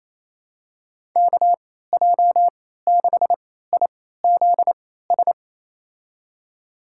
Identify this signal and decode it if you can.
Morse code, heard as a keyed tone, about 28 wpm, spaced as for 16 wpm, 700 Hz. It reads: KJ6IZS
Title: USA Call Signs - 6 Characters - 3x Character Spacing - 28wpm